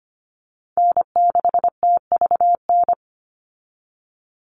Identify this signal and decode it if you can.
Morse code, heard as a keyed tone, about 25 wpm, 700 Hz. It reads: N6TVN
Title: Call Signs Repeated 2x - 25wpm